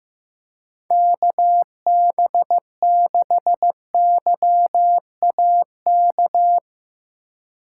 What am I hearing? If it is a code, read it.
Morse code, heard as a keyed tone, about 16 wpm, 700 Hz. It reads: KB6YAK
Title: Speed Racing - USA Call Signs 15wpm